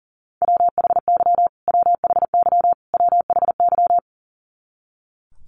Morse code, heard as a keyed tone, about 40 wpm, 700 Hz. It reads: WHY WHY WHY
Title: Top 300 Words Repeated 40wpm